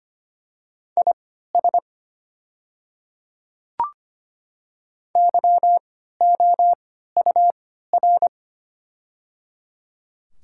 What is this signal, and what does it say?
Morse code, heard as a keyed tone, about 25 wpm, spaced as for 14 wpm, 700 Hz. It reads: IS YOUR
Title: 3x Character Spacing - Top 200 Words - 25wpm